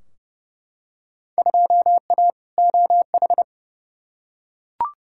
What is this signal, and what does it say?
Morse code, heard as a keyed tone, about 32 wpm, 700 Hz. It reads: 2A OH
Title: Field Day Exchanges - 30wpm